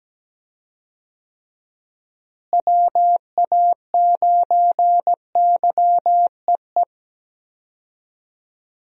Morse code, heard as a keyed tone, about 17 wpm, 700 Hz. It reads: WA9YEE
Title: USA Call Signs - 6 Characters - Repeated 3x - 17wpm